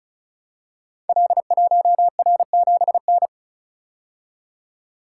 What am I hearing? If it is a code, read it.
Morse code, heard as a keyed tone, about 36 wpm, 700 Hz. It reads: L1R7N
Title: Code Groups - 5 Letter:Numbers - 35wpm